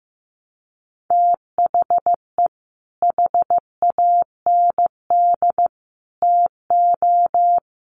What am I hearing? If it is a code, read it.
Morse code, heard as a keyed tone, about 15 wpm, 700 Hz. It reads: THE HAND TO
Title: Sentences from Top 1300 Words 15wpm